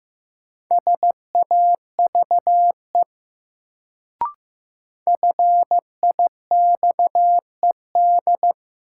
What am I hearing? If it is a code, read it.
Morse code, heard as a keyed tone, about 15 wpm, 700 Hz. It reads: SAVE FIXED
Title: Sentences from Top 1100 Words - New Word Drill - 15wpm